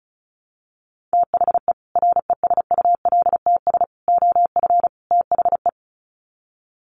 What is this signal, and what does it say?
Morse code, heard as a keyed tone, about 34 wpm, 700 Hz. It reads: THE RESULTS OF THE
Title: Sets of 4 Words 35wpm